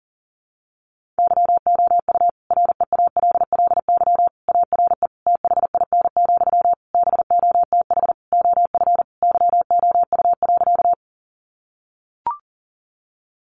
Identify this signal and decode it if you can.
Morse code, heard as a keyed tone, about 40 wpm, 700 Hz. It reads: YOU REALLY ARE THIN, BOTH OF YOU.
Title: Sentences from Top 1200 Words 40wpm